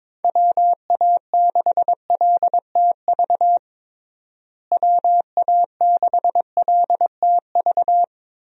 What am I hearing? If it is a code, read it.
Morse code, heard as a keyed tone, about 22 wpm, 700 Hz. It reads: WA6LTV WA6LTV
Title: USA Call Signs - 6 Characters - Repeated 3x - 22wpm